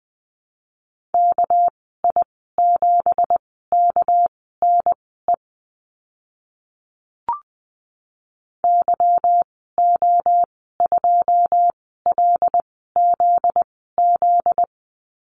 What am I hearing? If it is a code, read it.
Morse code, heard as a keyed tone, about 20 wpm, spaced as for 14 wpm, 700 Hz. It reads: KI7KNE YO2LZZ
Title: 245 Lesson 39g - Letter z - Call Signs 20wpm